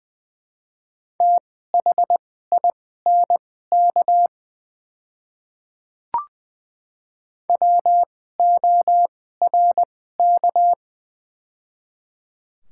Morse code, heard as a keyed tone, about 20 wpm, spaced as for 14 wpm, 700 Hz. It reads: THINK WORK
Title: 2x Character Spacing - Top 200 Words - 20wpm